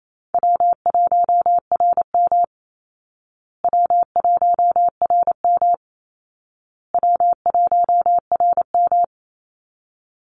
Morse code, heard as a keyed tone, about 28 wpm, 700 Hz. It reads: W1RM W1RM W1RM
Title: Easy Call Signs Repeated 3x - 28wpm